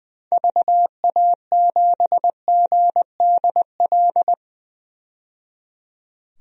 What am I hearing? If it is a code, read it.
Morse code, heard as a keyed tone, about 21 wpm, 700 Hz. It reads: VA7GDL
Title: DXCC-100 Call Signs - 6 Characters - 20wpm